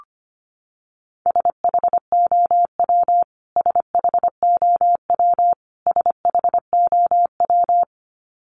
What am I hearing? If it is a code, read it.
Morse code, heard as a keyed tone, about 25 wpm, 700 Hz. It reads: SHOW SHOW SHOW
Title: Top 300 Words Repeated 25wpm